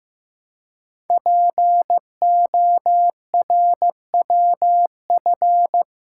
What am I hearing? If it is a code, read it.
Morse code, heard as a keyed tone, about 15 wpm, 700 Hz. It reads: PORWF
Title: Code Groups - 5 Letters - 15wpm